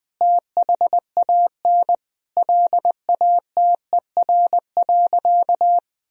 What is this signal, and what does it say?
Morse code, heard as a keyed tone, about 21 wpm, 700 Hz. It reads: THAN LATER.
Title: Sentences from Top 1000 Words - Review Part 4 - 20wpm